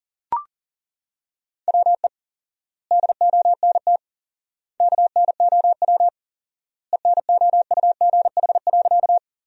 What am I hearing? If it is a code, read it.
Morse code, heard as a keyed tone, about 40 wpm, 700 Hz. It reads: WE DONT KNOW ENOUGH.
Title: 4x Word-Spacing - Sentences from Top 400 Words - 40wpm